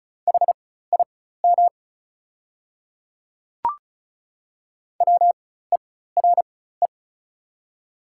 Morse code, heard as a keyed tone, about 34 wpm, spaced as for 16 wpm, 690 Hz. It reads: HIM WERE
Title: 4x Character Spacing - Top 200 Words - 35wpm